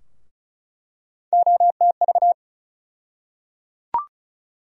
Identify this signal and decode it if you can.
Morse code, heard as a keyed tone, about 36 wpm, 700 Hz. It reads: OTV
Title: USA Call Sign Postfixes - 3 Letters 35wpm